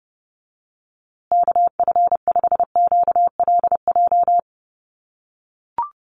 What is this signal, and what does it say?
Morse code, heard as a keyed tone, about 30 wpm, 700 Hz. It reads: KF5QLJ
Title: USA Call Signs Repeated 3x 30wpm